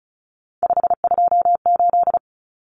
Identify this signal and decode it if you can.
Morse code, heard as a keyed tone, about 34 wpm, 700 Hz. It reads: 528
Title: Numbers - 3-Digit 35wpm